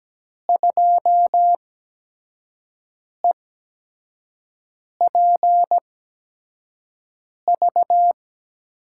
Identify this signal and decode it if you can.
Morse code, heard as a keyed tone, about 17 wpm, spaced as for 5 wpm, 700 Hz. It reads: 2EPV
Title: USA Call Signs - 5 Characters - 8x Character Spacing - 17wpm